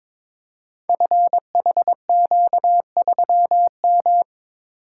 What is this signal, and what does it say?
Morse code, heard as a keyed tone, about 22 wpm, 700 Hz. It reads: FHQ3M
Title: Code Groups - 5 Letter:Numbers - 22wpm